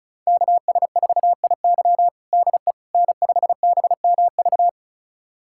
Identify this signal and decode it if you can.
Morse code, heard as a keyed tone, about 36 wpm, 700 Hz. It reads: KS4IY DE N5BMV
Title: USA Callsign DE Callsign - 5 Characters - 35wpm